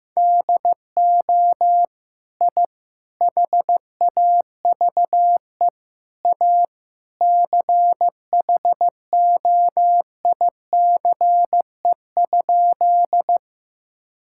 Morse code, heard as a keyed tone, about 15 wpm, 700 Hz. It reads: DO I HAVE A CHOICE?
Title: Sentences from Top 1000 Words - Review Part 6 - 15wpm